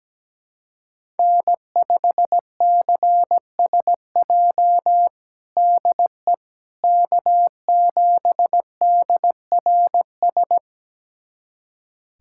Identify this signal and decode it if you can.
Morse code, heard as a keyed tone, about 17 wpm, 700 Hz. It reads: N5CSJ DE K7DRS
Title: USA Callsign DE Callsign - 5 Characters - 17wpm